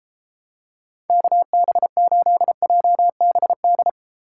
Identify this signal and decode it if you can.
Morse code, heard as a keyed tone, about 31 wpm, 700 Hz. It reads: KB8JBD